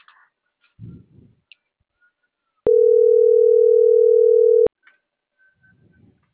Ringback tone (North America), 440 + 480 Hz.